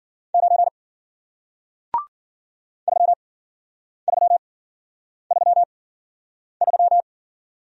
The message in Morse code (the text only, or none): 9 3333